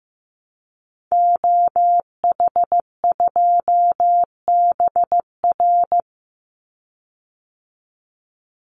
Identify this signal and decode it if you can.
Morse code, heard as a keyed tone, about 15 wpm, 700 Hz. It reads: OH2BR
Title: Call Signs Repeated 3x - 15wpm